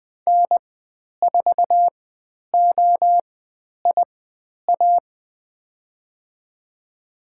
Morse code, heard as a keyed tone, about 20 wpm, spaced as for 10 wpm, 700 Hz. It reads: N4OIA
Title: 102 Lesson 17g - Letter u - Call Signs 20wpm